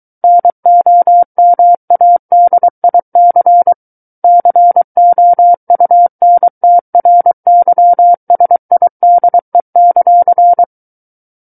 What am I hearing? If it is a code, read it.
Morse code, heard as a keyed tone, about 23 wpm, 700 Hz. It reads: NOMADIC COUNTRYSIDE;
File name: Book One / 14-Chapter 14 - 23 WPM